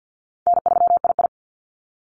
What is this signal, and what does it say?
Morse code, heard as a keyed tone, about 50 wpm, 700 Hz. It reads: N3II
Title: NA SOTA Activator Callsigns 50wpm